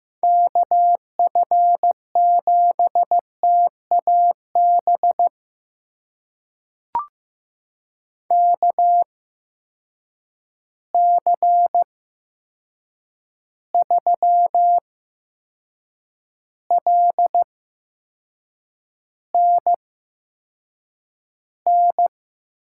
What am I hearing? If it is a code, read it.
Morse code, heard as a keyed tone, about 15 wpm, spaced as for 6 wpm, 700 Hz. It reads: KF7TAB KC3LNN
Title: USA Call Signs - 6 Characters - 8x Character Spacing - 15wpm